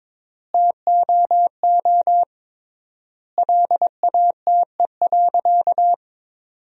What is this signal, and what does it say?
Morse code, heard as a keyed tone, about 22 wpm, 700 Hz. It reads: TOO LATE.
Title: 3x Word-Spacing - Sentences from Top 300 Words - 22wpm